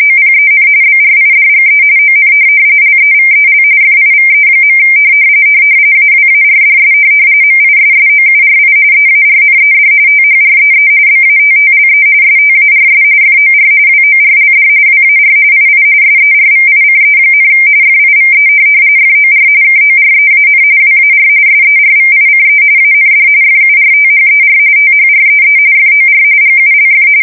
RTTY, about 45.45 baud. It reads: OPEN BRIDGE ON LEVEL FIVE. AWARDS MISSOURI EARNED THREE BATTLE STARS FOR WORLD WAR II SERVICE, FIVE FOR KOREAN WAR SERVICE AND FURTHER THREE FOR GULF
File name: fsk170 / USS-Missouri